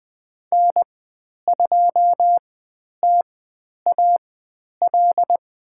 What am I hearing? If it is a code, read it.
Morse code, heard as a keyed tone, about 20 wpm, spaced as for 10 wpm, 700 Hz. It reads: N2TAL
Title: 095 Lesson 16g - Letter c - Call Signs 20wpm